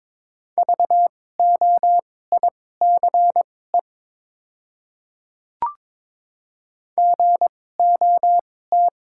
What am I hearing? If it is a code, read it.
Morse code, heard as a keyed tone, about 22 wpm, spaced as for 16 wpm, 690 Hz. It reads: VOICE GOT
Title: 2x Character Spacing - Top 500 Words - 22wpm